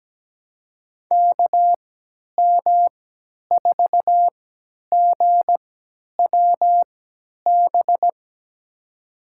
Morse code, heard as a keyed tone, about 17 wpm, spaced as for 10 wpm, 700 Hz. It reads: KM4GWB
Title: USA Call Signs - 6 Characters - 3x Character Spacing - 17wpm